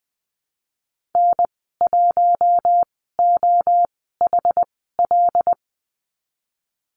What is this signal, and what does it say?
Morse code, heard as a keyed tone, about 20 wpm, spaced as for 15 wpm, 700 Hz. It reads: N1OHL
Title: 186 Lesson 30g - Letter b - Call Signs 20wpm